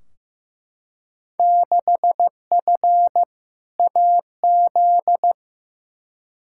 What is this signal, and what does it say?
Morse code, heard as a keyed tone, about 15 wpm, 700 Hz. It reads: 6F AZ
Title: Field Day Exchanges Encore - 15wpm